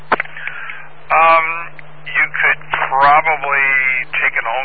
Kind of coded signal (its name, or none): Emergency Alert System (SAME) tones